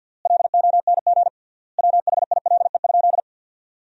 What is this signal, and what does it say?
Morse code, heard as a keyed tone, about 47 wpm, 700 Hz. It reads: LONG WHILE?
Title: 3x Word-Spacing - Sentences from Top 300 Words - 50wpm